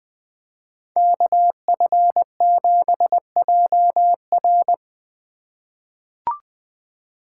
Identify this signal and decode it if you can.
Morse code, heard as a keyed tone, about 21 wpm, 700 Hz. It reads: KF7JR
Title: USA Call Signs - 2x Character Spacing - 20wpm